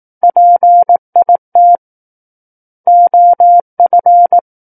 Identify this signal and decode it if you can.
Morse code, heard as a keyed tone, about 18 wpm, 700 Hz. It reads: PIT OF
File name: 18 WPM